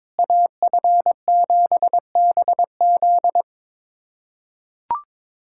Morse code, heard as a keyed tone, about 22 wpm, 700 Hz. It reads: AF7BZ
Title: USA Callsign DE Callsign - 5 Characters - 22wpm